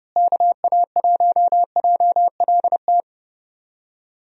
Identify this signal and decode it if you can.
Morse code, heard as a keyed tone, about 30 wpm, 700 Hz. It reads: KA1JLT
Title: USA Call Signs - 6 Characters - Repeated 2x - 30wpm